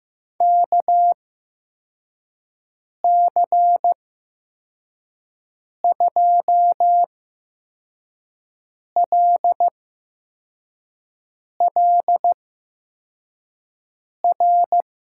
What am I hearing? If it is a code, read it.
Morse code, heard as a keyed tone, about 15 wpm, spaced as for 4 wpm, 700 Hz. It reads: KC2LLR
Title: USA Call Signs - 6 Characters - 8x Character Spacing - 15wpm